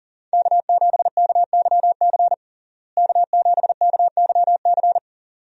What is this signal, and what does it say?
Morse code, heard as a keyed tone, about 40 wpm, 700 Hz. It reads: K7KYC K7KYC
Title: USA Call Signs - 5 Characters - Repeated 2x - 40wpm